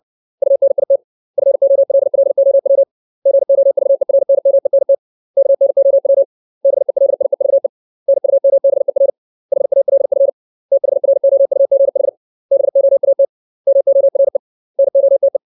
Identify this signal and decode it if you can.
Morse code, heard as a keyed tone, about 60 wpm, 550 Hz. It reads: UNIT FOLLOW GOVERNMENT KNOW BELIEVE NUMBER HALF THROUGH BOAT MORE NOTE
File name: 632 words / 632 Words 60WPM